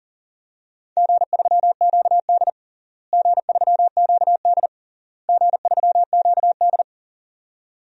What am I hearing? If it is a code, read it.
Morse code, heard as a keyed tone, about 40 wpm, 700 Hz. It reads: G3QD G3QD G3QD
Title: DXCC-100 Call Signs - 4 Characters - Repeated 3x - 40wpm